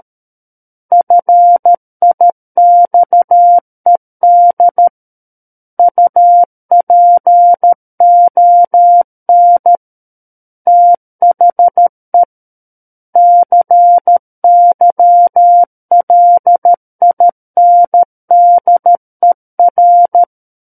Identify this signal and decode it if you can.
Morse code, heard as a keyed tone, about 13 wpm, 700 Hz. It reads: FIXED UPON THE CYLINDER
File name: Book One / 04-Chapter 4 - 13 WPM